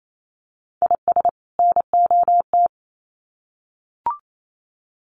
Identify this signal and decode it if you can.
Morse code, heard as a keyed tone, about 28 wpm, 700 Hz. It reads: IS NOT
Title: Sets of 2 Words 28wpm